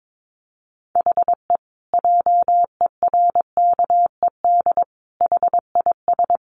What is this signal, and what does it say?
Morse code, heard as a keyed tone, about 22 wpm, 700 Hz. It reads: HE JERKED HIS
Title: A Sound Of Thunder 22wpm